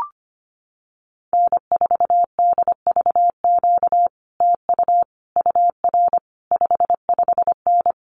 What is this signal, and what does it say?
Morse code, heard as a keyed tone, about 25 wpm, 700 Hz. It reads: N4DVQ TU UR 55N
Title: Common QSO Phrases Encore 25wpm